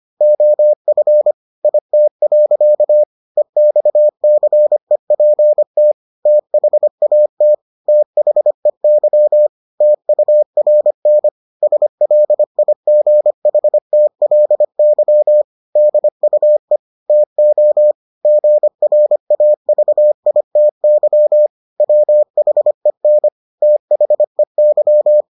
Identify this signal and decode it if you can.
Morse code, heard as a keyed tone, about 25 wpm, 600 Hz. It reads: OF IT. EXCEPT THAT THEY TURN SLIGHTLY DUE TO GRAVITY WHEN THEY